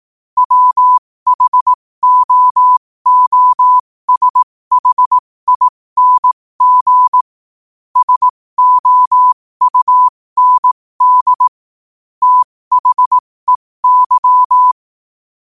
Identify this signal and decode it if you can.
Morse code, heard as a keyed tone, about 18 wpm, spaced as for 15 wpm, 1000 Hz. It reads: WHOOSHING SOUND THEY
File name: qotdfiles / 15wpm